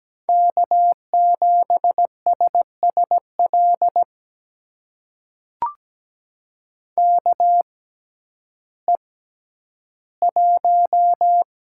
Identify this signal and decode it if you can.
Morse code, heard as a keyed tone, about 17 wpm, spaced as for 10 wpm, 700 Hz. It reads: K7SSL KE1
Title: USA Call Signs - 5 Characters - 6x Character Spacing - 17wpm